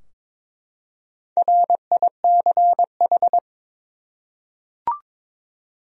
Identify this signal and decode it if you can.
Morse code, heard as a keyed tone, about 22 wpm, 700 Hz. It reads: RICH
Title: USA Common Names - Max 5 Letters - 22wpm